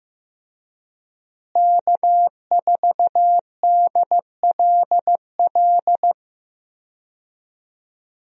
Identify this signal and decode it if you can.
Morse code, heard as a keyed tone, about 15 wpm, 700 Hz. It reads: K4DLL
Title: USA Call Signs - 5 Characters - Repeated 2x - 15wpm